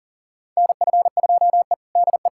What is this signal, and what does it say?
Morse code, heard as a keyed tone, about 40 wpm, 700 Hz. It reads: NF2E DE
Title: USA Callsign DE Callsign - 4 Characters - 40wpm